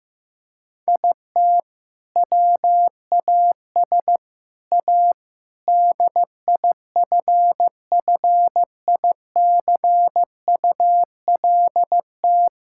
Morse code, heard as a keyed tone, about 15 wpm, 700 Hz. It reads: IT WAS A DIFFICULT
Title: Sentences from Top 1000 Words - Review Part 6 - 15wpm